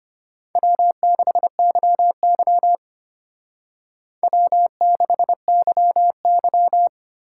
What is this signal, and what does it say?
Morse code, heard as a keyed tone, about 32 wpm, 700 Hz. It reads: W6YY W6YY